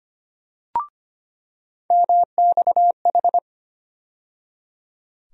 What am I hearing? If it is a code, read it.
Morse code, heard as a keyed tone, about 25 wpm, 700 Hz. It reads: MXH
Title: USA Call Sign Postfixes - 3 Letters 25wpm